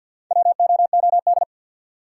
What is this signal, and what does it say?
Morse code, heard as a keyed tone, about 50 wpm, 700 Hz. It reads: WOOD
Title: Four Letter Words - 50wpm